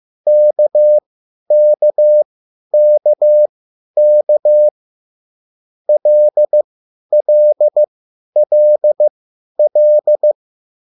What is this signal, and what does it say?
Morse code, heard as a keyed tone, about 15 wpm, spaced as for 10 wpm, 600 Hz. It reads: KKKK LLLL
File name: QSL / CW lesson